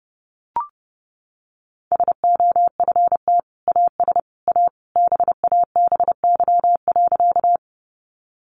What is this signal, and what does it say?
Morse code, heard as a keyed tone, about 30 wpm, 700 Hz. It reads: SOFT AS A BABY.
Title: Sentences from Top 1500 Words 30wpm